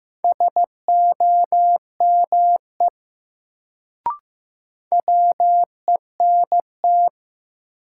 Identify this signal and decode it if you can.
Morse code, heard as a keyed tone, about 15 wpm, 700 Hz. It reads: SOME WENT
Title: Sets of 2 Words - New Word Drill - 15wpm